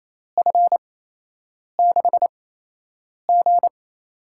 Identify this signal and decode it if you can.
Morse code, heard as a keyed tone, about 28 wpm, spaced as for 8 wpm, 700 Hz. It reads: F6G
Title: USA Call Signs - 4 Characters - 8x Character Spacing - 28wpm